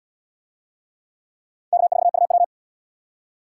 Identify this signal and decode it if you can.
Morse code, heard as a keyed tone, about 77 wpm, 700 Hz. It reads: K4UW